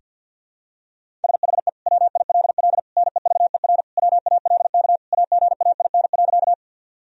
Morse code, heard as a keyed tone, about 51 wpm, 700 Hz. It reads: SHE WILL NEVER WALK AGAIN.
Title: Sentences from Top 1000 Words - Review Part 1 - 50wpm